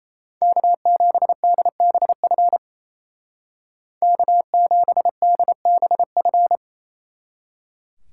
Morse code, heard as a keyed tone, about 34 wpm, 700 Hz. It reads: K7DBF K7DBF
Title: Speed Racing - USA 1x3 Call Signs 28wpm